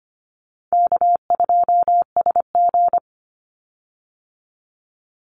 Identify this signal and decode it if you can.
Morse code, heard as a keyed tone, about 25 wpm, 700 Hz. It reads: K2SG
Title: Easy Call Signs - 25wpm